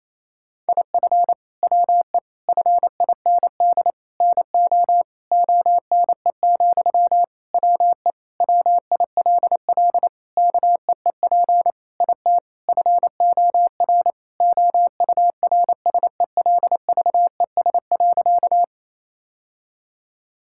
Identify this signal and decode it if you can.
Morse code, heard as a keyed tone, about 28 wpm, 700 Hz. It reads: IF WE FIND NO ONE, WE WILL KEEP IT FOR OURSELVES.
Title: Sentences from Top 1600 Words 28wpm